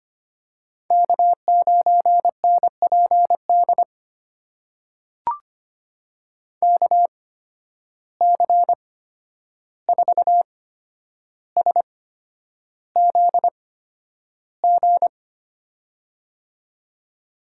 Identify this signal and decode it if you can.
Morse code, heard as a keyed tone, about 25 wpm, spaced as for 10 wpm, 700 Hz. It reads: K9NPD KC4SZG
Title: USA Call Signs - 8x Character Spacing - 25wpm